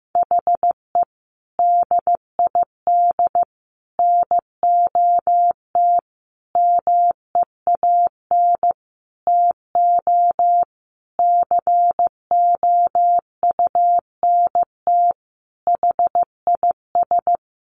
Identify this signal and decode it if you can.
Morse code, heard as a keyed tone, about 15 wpm, 700 Hz. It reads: HE DID NOT MEAN TO COUNT HIS
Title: Sentences from Top 1500 Words 15wpm